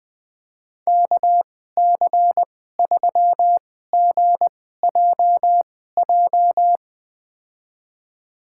Morse code, heard as a keyed tone, about 20 wpm, spaced as for 14 wpm, 700 Hz. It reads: KC3GJJ
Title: USA Call Signs - 6 Characters - 2x Character Spacing - 20wpm